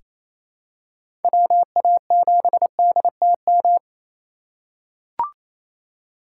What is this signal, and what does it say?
Morse code, heard as a keyed tone, about 28 wpm, 700 Hz. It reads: WA7DTM
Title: USA Call Signs - 6 Characters - 6x Character Spacing - 28wpm